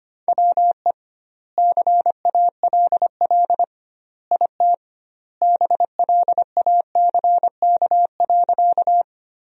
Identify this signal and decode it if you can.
Morse code, heard as a keyed tone, about 25 wpm, 700 Hz. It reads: WE CALL IT BLACK.
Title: 2x Word-Spacing - Sentences from Top 400 Words - 25wpm